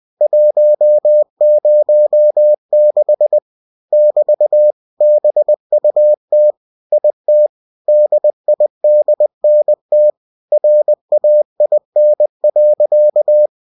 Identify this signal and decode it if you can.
Morse code, heard as a keyed tone, about 20 wpm, 600 Hz. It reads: 106 <BT> BUT IT DIDNT RAIN.
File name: cw-books / winnie-the-pooh / winnie-20wpm-page-0106